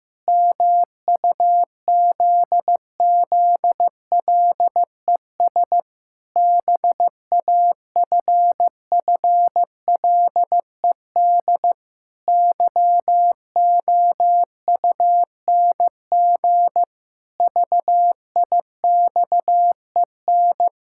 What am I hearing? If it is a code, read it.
Morse code, heard as a keyed tone, about 15 wpm, 700 Hz. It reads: MUZZLES BAFFLED YOUNG VIXEN
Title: Pangram Challenge v2 - 15wpm